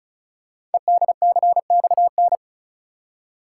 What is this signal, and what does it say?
Morse code, heard as a keyed tone, about 36 wpm, 700 Hz. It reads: EDCXN
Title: Code Groups - 5 Letters - 35wpm